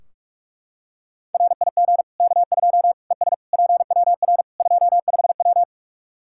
Morse code, heard as a keyed tone, about 44 wpm, 700 Hz. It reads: RIG K2 ES PWR 25W
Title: Practice QSO - Exchange 2 - Info - 45wpm